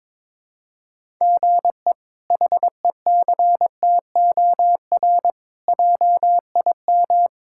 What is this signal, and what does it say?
Morse code, heard as a keyed tone, about 22 wpm, 700 Hz. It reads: GE HECTOR JIM
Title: K1USN SST QSO Practice - 22wpm